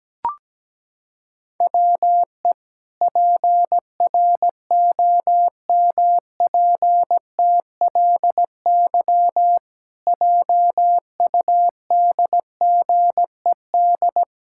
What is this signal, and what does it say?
Morse code, heard as a keyed tone, about 17 wpm, 700 Hz. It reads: WE PROMPTLY JUDGED
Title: Pangram Challenge v2 - 17wpm